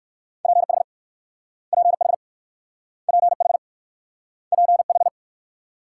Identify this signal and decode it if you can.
Morse code, heard as a keyed tone, about 55 wpm, 690 Hz. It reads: PH PH PH PH
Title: Speed-Racing Multi-letter Phonemes - 45wpm